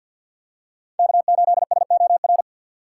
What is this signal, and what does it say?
Morse code, heard as a keyed tone, about 53 wpm, 700 Hz. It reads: K8SOR
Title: USA Call Signs - 5 Characters - 2x Character Spacing - 50wpm